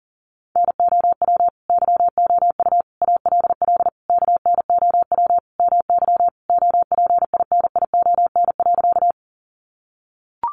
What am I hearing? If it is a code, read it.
Morse code, heard as a keyed tone, about 40 wpm, 700 Hz. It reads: NOW YOU ALL KNOW MY OPINION.
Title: Sentences from Top 1200 Words 40wpm